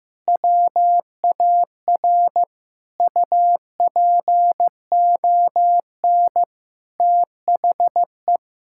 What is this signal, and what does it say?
Morse code, heard as a keyed tone, about 15 wpm, 700 Hz. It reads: WAR UPON THE
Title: Sentences from Top 1000 Words - Review Part 6 - 15wpm